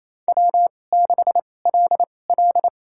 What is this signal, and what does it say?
Morse code, heard as a keyed tone, about 28 wpm, spaced as for 20 wpm, 700 Hz. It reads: W6LL
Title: USA Call Signs - 4 Characters - 2x Character Spacing - 28wpm